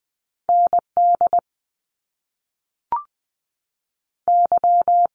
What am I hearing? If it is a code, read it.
Morse code, heard as a keyed tone, about 20 wpm, 700 Hz. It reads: ND Y
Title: Top Two Letter Combinations in Words 20wpm